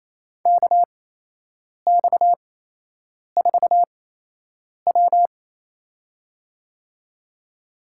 Morse code, heard as a keyed tone, about 28 wpm, spaced as for 8 wpm, 700 Hz. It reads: KX4W
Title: USA Call Signs - 8x Character Spacing - 28wpm